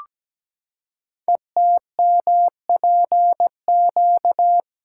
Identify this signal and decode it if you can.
Morse code, heard as a keyed tone, about 17 wpm, 700 Hz. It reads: ETMPQ
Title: Code Groups - 5 Letters - 17wpm